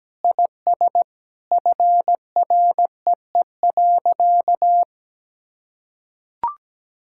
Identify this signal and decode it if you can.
Morse code, heard as a keyed tone, about 17 wpm, 700 Hz. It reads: IS FREE.
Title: Sentences from Top 1000 Words - Review Part 6 - 17wpm